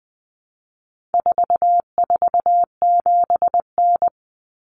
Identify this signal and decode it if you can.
Morse code, heard as a keyed tone, about 20 wpm, 700 Hz. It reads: 447N
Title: Common Cut Numbers - 4-Digit 20wpm